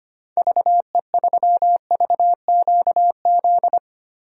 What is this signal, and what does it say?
Morse code, heard as a keyed tone, about 25 wpm, 700 Hz. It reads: VE3VQZ
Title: DXCC-100 Call Signs - 6 Characters - Repeated 2x - 25wpm